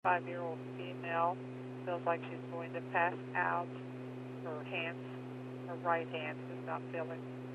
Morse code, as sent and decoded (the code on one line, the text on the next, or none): none